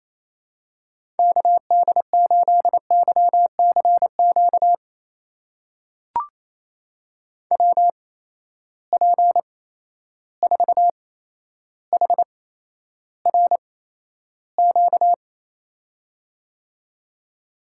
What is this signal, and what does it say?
Morse code, heard as a keyed tone, about 27 wpm, spaced as for 12 wpm, 700 Hz. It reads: KD8YCQ WP4HRQ
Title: USA Call Signs - 6 Characters - 8x Character Spacing - 28wpm